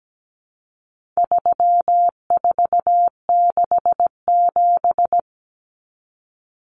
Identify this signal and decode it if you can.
Morse code, heard as a keyed tone, about 17 wpm, 700 Hz. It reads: 3467
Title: Numbers - 4-Digit 17wpm